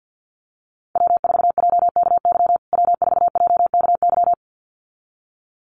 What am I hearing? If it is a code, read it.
Morse code, heard as a keyed tone, about 51 wpm, 700 Hz. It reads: W4JKY W4JKY
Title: USA Call Signs Repeated 2x 50wpm